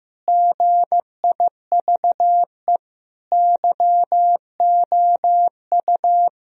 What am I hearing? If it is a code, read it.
Morse code, heard as a keyed tone, about 16 wpm, 700 Hz. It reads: GIVE YOU
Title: Sentences from Top 1000 Words - Review Part 2 - 15wpm